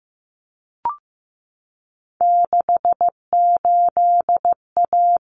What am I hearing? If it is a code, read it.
Morse code, heard as a keyed tone, about 15 wpm, 700 Hz. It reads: 68A
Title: Common Cut Numbers - 3-Digit 15wpm